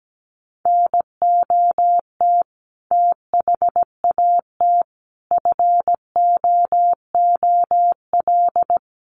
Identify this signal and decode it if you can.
Morse code, heard as a keyed tone, about 17 wpm, 700 Hz. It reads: NOT THAT FOOL
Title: A Sound Of Thunder 17wpm